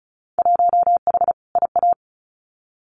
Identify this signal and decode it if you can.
Morse code, heard as a keyed tone, about 36 wpm, 700 Hz. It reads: 1H IA
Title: Winter Field Day Exchanges 35wpm